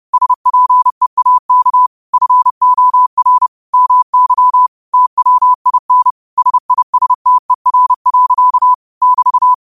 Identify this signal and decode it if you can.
Morse code, heard as a keyed tone, about 30 wpm, 1000 Hz. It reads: SPEAK FOR MY TWIN SISTER. <BT>